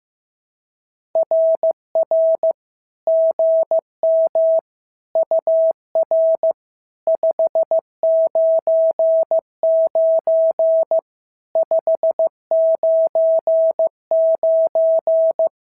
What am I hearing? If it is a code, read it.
Morse code, heard as a keyed tone, about 15 wpm, 650 Hz. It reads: RR GM UR 599 599
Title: SOTA QSO Practice 15wpm